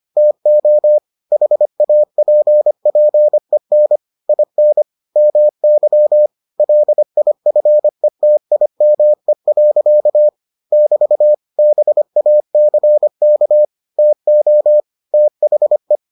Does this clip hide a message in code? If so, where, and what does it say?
Yes: Morse code from 0.2 to 16.0 s, heard as a keyed tone, about 25 wpm, 600 Hz. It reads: TO HAPPEN IN MY LIFETIME. <BT> BACK TO THE